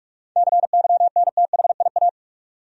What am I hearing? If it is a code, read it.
Morse code, heard as a keyed tone, about 46 wpm, 700 Hz. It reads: CYNTHIA
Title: USA Common Names - 45wpm